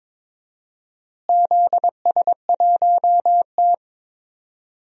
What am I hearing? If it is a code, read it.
Morse code, heard as a keyed tone, about 22 wpm, 700 Hz. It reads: ZS1T
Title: DXCC-100 Call Signs - 4 Characters - Repeated 3x - 22wpm